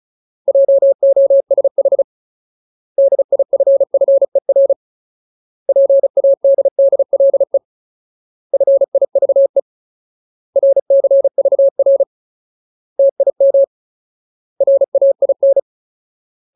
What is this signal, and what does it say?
Morse code, heard as a keyed tone, about 34 wpm, 550 Hz. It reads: JOSH DIFFER PADDLE FIVE RCVR TIM RAIN
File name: Ultimate Word List / 35WPM Ultimate